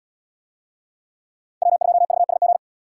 Morse code, heard as a keyed tone, about 68 wpm, 700 Hz. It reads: W2VSG